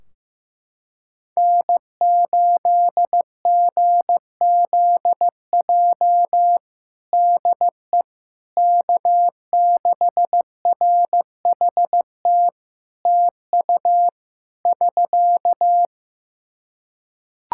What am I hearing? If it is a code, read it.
Morse code, heard as a keyed tone, about 15 wpm, 700 Hz. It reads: N8GZJ DE K6RHT TU <SK>
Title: Practice QSO - Exchange 4 - Ending - 15wpm